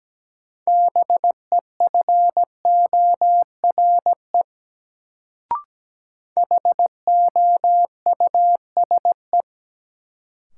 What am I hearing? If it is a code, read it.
Morse code, heard as a keyed tone, about 17 wpm, 700 Hz. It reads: BEFORE HOUSE
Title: Sets of 2 Words - New Word Drill - 17wpm